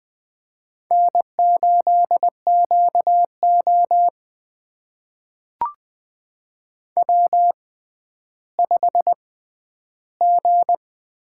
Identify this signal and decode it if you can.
Morse code, heard as a keyed tone, about 20 wpm, spaced as for 11 wpm, 700 Hz. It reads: N8QO W5G
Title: USA Call Signs - 4 Characters - 6x Character Spacing - 20wpm